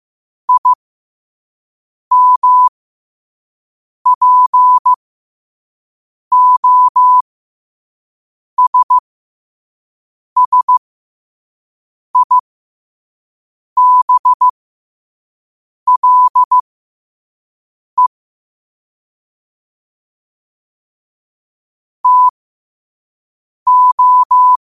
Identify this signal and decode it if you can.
Morse code, heard as a keyed tone, about 15 wpm, spaced as for 5 wpm, 1000 Hz. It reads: IMPOSSIBLE TO
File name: qotdfiles / 5wpm